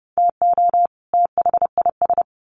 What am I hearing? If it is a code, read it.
Morse code, heard as a keyed tone, about 28 wpm, 700 Hz. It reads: TO THIS
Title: Sets of 2 Words 30wpm